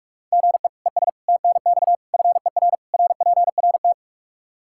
Morse code, heard as a keyed tone, about 42 wpm, 700 Hz. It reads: GE ES TNX FER RPRT
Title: Practice QSO - Exchange 1 - Beginning - 45wpm